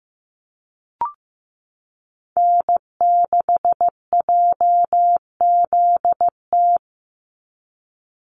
Morse code, heard as a keyed tone, about 15 wpm, 700 Hz. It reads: N6JZT
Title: NA SOTA Activator Callsigns 15wpm